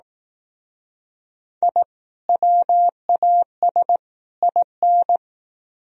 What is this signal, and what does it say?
Morse code, heard as a keyed tone, about 18 wpm, 700 Hz. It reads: I WAS IN
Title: Speed-Racing - Sets of 3 Words 15wpm